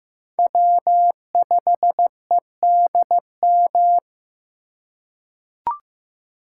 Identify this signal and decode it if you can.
Morse code, heard as a keyed tone, about 15 wpm, 700 Hz. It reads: W5EDM
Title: USA Call Signs - 5 Characters - 3x Character Spacing - 15wpm